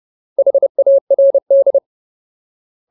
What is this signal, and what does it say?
Morse code, heard as a keyed tone, about 30 wpm, 550 Hz. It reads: HARD
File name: Ultimate Word List / 30WPM Ultimate